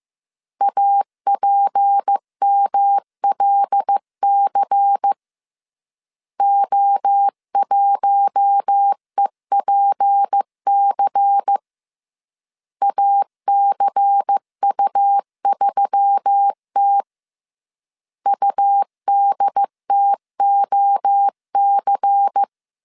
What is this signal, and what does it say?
Morse code, heard as a keyed tone, about 15 wpm, 800 Hz. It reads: APMLC O1EPC ACU3T UDTOC
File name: CW / DET / 1 6 1500C S C 14wpm 2951